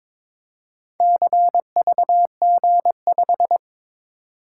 Morse code, heard as a keyed tone, about 22 wpm, 700 Hz. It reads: CVG5
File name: practice / 22wpm / Four Letter-Number Combinations - 22wpm